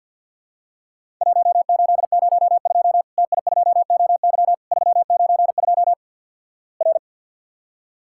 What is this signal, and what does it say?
Morse code, heard as a keyed tone, about 50 wpm, 700 Hz. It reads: 1802 TI2OY 392 R
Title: Worked All Europe DX Contest - QTC Exchange - 50wpm